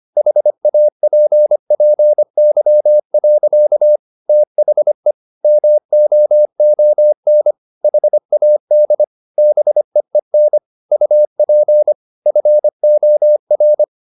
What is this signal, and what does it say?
Morse code, heard as a keyed tone, about 25 wpm, 600 Hz. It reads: HAPPY. THE MOON HAD BEEN UP FOR